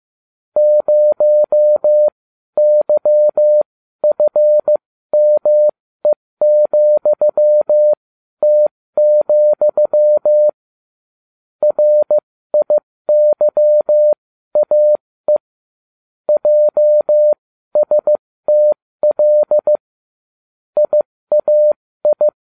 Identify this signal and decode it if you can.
Morse code, heard as a keyed tone, about 15 wpm, spaced as for 13 wpm, 600 Hz. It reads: 0YFME,T, RIYAE JSTL IAI